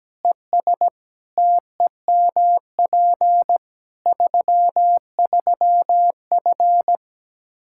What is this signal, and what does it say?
Morse code, heard as a keyed tone, about 17 wpm, 700 Hz. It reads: ES TEMP 33F
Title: K1USN SST QSO Practice 17wpm